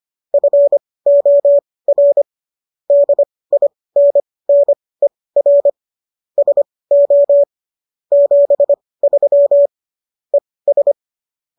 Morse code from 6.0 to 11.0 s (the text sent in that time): SO 73 ES